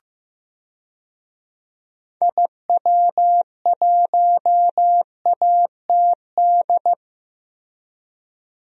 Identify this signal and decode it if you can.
Morse code, heard as a keyed tone, about 14 wpm, 700 Hz. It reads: IW1ATD
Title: DXCC-100 Call Signs - 6 Characters - Repeated 3x - 15wpm